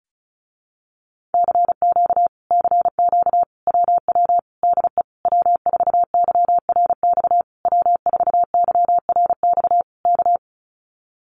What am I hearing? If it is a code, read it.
Morse code, heard as a keyed tone, about 36 wpm, 700 Hz. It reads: CQ CQ WW DE W4YRX W4YRX K
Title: Common QSO Phrases Encore 35wpm